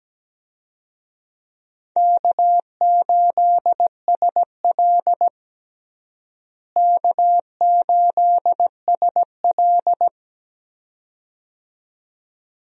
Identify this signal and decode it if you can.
Morse code, heard as a keyed tone, about 17 wpm, 700 Hz. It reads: K8SL K8SL
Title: USA Call Signs - 4 Characters - Repeated 2x - 17wpm